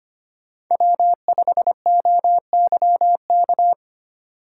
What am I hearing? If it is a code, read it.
Morse code, heard as a keyed tone, about 25 wpm, 700 Hz. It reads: W5OYK
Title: USA Call Signs - 5 Characters - 2x Character Spacing - 25wpm